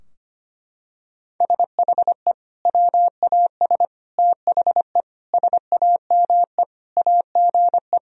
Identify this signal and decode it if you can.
Morse code, heard as a keyed tone, about 25 wpm, 700 Hz. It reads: SHE WAS THE SAME AGE
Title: Sentences from Top 1000 Words - Review Part 3 - 25wpm